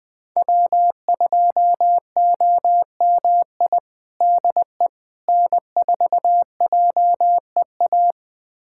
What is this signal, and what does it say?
Morse code, heard as a keyed tone, about 20 wpm, 700 Hz. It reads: W2OMI DE N4JEA
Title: USA Callsign DE Callsign - 5 Characters - 20wpm